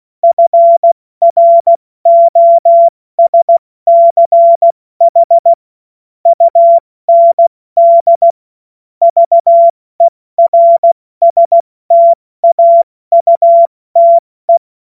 Morse code, heard as a keyed tone, about 16 wpm, 680 Hz. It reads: FROSCH UND VERSTAUTE
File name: Morsekurs / Witze / Witz 08. November 2022 70 F80